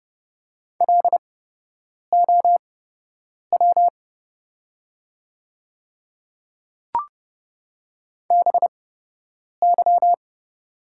Morse code, heard as a keyed tone, about 30 wpm, spaced as for 8 wpm, 700 Hz. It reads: LOW BY
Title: 8x Character Spacing - Top 200 Words - 30wpm